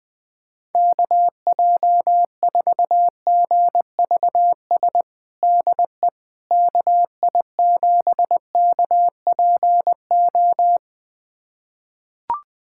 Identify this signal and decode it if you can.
Morse code, heard as a keyed tone, about 20 wpm, 700 Hz. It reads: KJ4GVS DE KI7KPO
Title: USA Callsign DE Callsign - 6 Characters - 20wpm